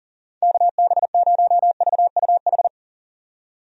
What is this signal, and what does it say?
Morse code, heard as a keyed tone, about 40 wpm, 700 Hz. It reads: KB0VUH
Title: USA Call Signs - 6 Characters - 40wpm